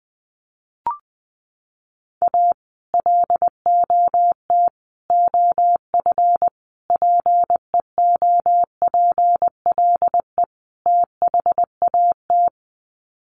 Morse code, heard as a keyed tone, about 20 wpm, 700 Hz. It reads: A LOT OF PEOPLE THAT
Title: Sets of 5 Words Encore2 20wpm